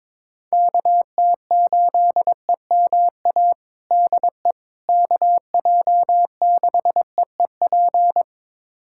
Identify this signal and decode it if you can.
Morse code, heard as a keyed tone, about 22 wpm, 700 Hz. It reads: KT8EMA DE KJ6EEP
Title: USA Callsign DE Callsign 22wpm